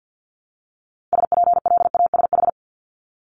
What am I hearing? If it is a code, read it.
Morse code, heard as a keyed tone, about 50 wpm, 700 Hz. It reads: SPLASH
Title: Multi-letter Phonemes Repeated 3x 50wpm